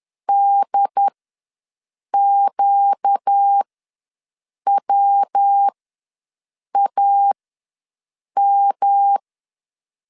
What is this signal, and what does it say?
Morse code, heard as a keyed tone, about 10 wpm, spaced as for 6 wpm, 800 Hz. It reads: DQWAM
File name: LEZIONI / 5 LBVC / 5 LB ws 10wpm 3616